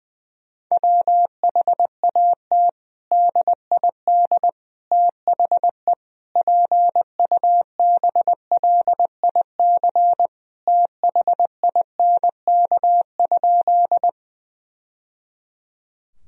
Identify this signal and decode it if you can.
Morse code, heard as a keyed tone, about 20 wpm, 700 Hz. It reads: WHAT DID THE PUBLIC THINK?
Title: Sentences from Top 1000 Words - Review Part 3 - 20wpm